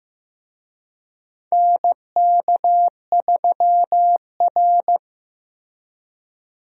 Morse code, heard as a keyed tone, about 15 wpm, 700 Hz. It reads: NK3R
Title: USA Call Signs - 4 Characters - Repeated 3x - 15wpm